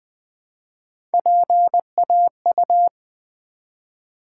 Morse code, heard as a keyed tone, about 20 wpm, 700 Hz. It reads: PAU
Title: Build A Word - 1 Letter At A Time - 5-Letter Words - 20wpm